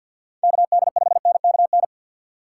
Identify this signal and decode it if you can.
Morse code, heard as a keyed tone, about 52 wpm, 700 Hz. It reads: KD5NKN
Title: USA Call Signs - 3x Character Spacing - 50wpm